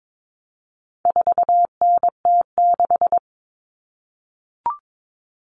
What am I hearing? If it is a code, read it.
Morse code, heard as a keyed tone, about 22 wpm, 700 Hz. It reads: 4NT6
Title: All Cut Numbers - 4-Digit 22wpm